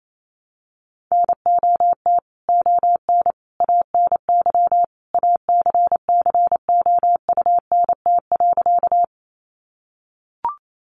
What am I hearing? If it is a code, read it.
Morse code, heard as a keyed tone, about 28 wpm, 700 Hz. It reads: NOT ON ANY ACCOUNT.
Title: Sentences from Top 900 Words 28wpm